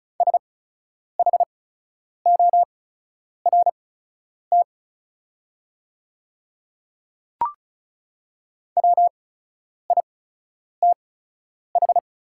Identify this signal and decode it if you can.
Morse code, heard as a keyed tone, about 34 wpm, spaced as for 10 wpm, 690 Hz. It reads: SHORT WITH
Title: 8x Character Spacing - Top 500 Words - 35wpm